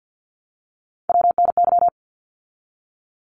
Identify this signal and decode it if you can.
Morse code, heard as a keyed tone, about 50 wpm, 700 Hz. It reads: WNY
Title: ARRL Sections 50wpm